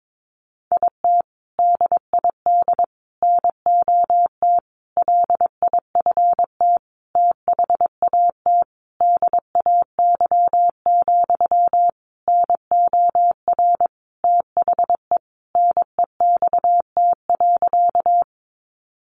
Morse code, heard as a keyed tone, about 22 wpm, 700 Hz. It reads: IT DID NOT LIFT THAT DAY, NOR THE NEXT.
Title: Sentences from Top 1500 Words 22wpm